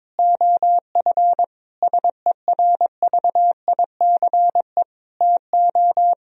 Morse code, heard as a keyed tone, about 22 wpm, 700 Hz. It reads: OF SERVICE TO
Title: Sentences from Top 1000 Words - Review Part 4 - 22wpm